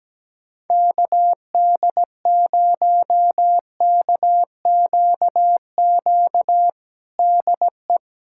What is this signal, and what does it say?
Morse code, heard as a keyed tone, about 17 wpm, 700 Hz. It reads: KD0KQQ DE
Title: USA Callsign DE Callsign - 6 Characters - 17wpm